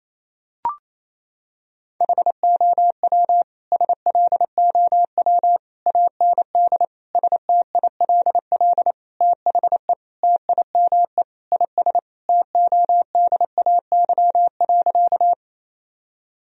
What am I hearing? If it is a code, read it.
Morse code, heard as a keyed tone, about 28 wpm, 700 Hz. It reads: HOW SLOW AND STILL THE TIME IS TODAY.
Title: Sentences from Top 1000 Words - Review Part 2 - 28wpm